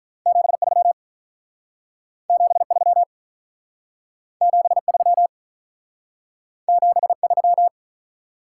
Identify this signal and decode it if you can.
Morse code, heard as a keyed tone, about 43 wpm, 700 Hz. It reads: 73 73 73 73
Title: Speed-Racing Top 180 QSO Elements 35wpm